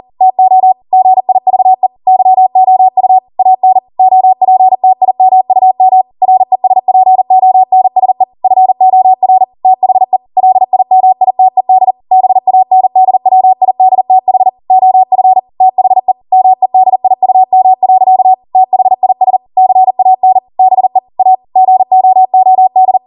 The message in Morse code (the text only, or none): TO GIVE YOU AN OPTIMUM RESPONSE FOR THE LIMITED BANDWIDTH OF THE MEDIUM. THIS CAN BE A GOOD